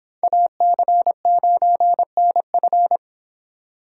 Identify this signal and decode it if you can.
Morse code, heard as a keyed tone, about 25 wpm, 700 Hz. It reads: AC9NF